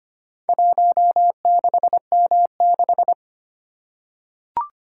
Morse code, heard as a keyed tone, about 25 wpm, 700 Hz. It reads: 16M6
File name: practice / 25wpm / Four Letter-Number Combinations - 25wpm